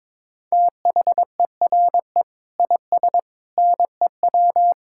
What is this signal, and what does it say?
Morse code, heard as a keyed tone, about 22 wpm, 700 Hz. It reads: THERE IS NEW
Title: Sentences from Top 1600 Words 22wpm